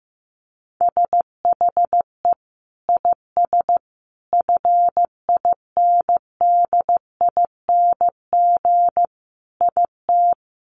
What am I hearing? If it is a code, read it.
Morse code, heard as a keyed tone, about 15 wpm, 700 Hz. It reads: SHE IS FINDING IT
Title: Sentences from Top 1100 Words 15wpm